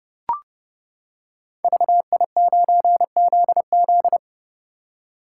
Morse code, heard as a keyed tone, about 30 wpm, 700 Hz. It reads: VI9ZZ
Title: Code Groups - 5 Letter:Numbers - 30wpm